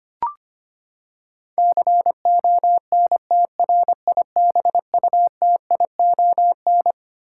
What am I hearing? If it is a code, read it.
Morse code, heard as a keyed tone, about 25 wpm, 700 Hz. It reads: CONTRIBUTION
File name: Twelve Letter Words - 25wpm